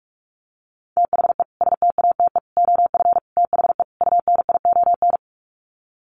Morse code, heard as a keyed tone, about 44 wpm, 700 Hz. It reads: THE STATE OF THE UNION
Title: Sets of 5 Words Encore 45wpm